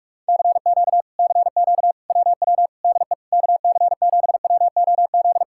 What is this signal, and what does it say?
Morse code, heard as a keyed tone, about 37 wpm, 700 Hz. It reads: CQ CQ WW DE KC7WOZ
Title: Speed-Racing Common QSO Phrases - 30wpm